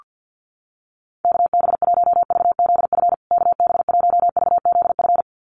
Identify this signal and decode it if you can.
Morse code, heard as a keyed tone, about 48 wpm, 700 Hz. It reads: KB1VZF KB1VZF
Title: USA Call Signs Repeated 2x 50wpm